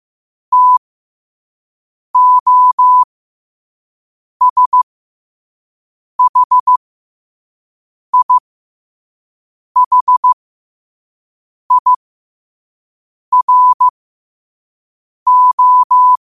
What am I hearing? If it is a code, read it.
Morse code, heard as a keyed tone, about 15 wpm, spaced as for 5 wpm, 1000 Hz. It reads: TOSHIHIRO
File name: qotdfiles / 5wpm